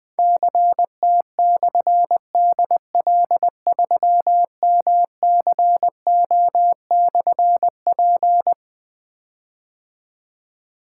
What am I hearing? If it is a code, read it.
Morse code, heard as a keyed tone, about 20 wpm, 700 Hz. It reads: CT/DL3MCO/P
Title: Difficult Call Signs 20wpm